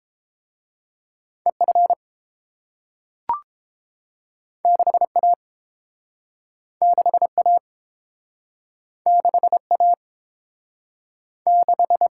Morse code, heard as a keyed tone, about 29 wpm, 700 Hz. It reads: EF 6A 6A 6A 6